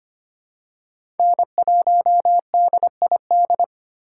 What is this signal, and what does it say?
Morse code, heard as a keyed tone, about 25 wpm, 700 Hz. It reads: N1DID
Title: K1USN SST QSO Practice - 25wpm